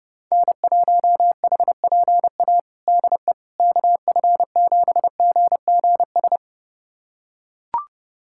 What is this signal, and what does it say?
Morse code, heard as a keyed tone, about 32 wpm, 700 Hz. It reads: N1HPA DE KF7GGS
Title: USA Callsign DE Callsign 30wpm